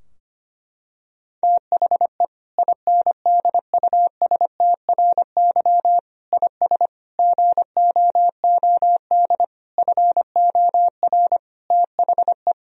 Morse code, heard as a keyed tone, about 25 wpm, 700 Hz. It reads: THE INDUSTRY IS GOOD FOR THE
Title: Sentences from Top 1000 Words - Review Part 4 - 25wpm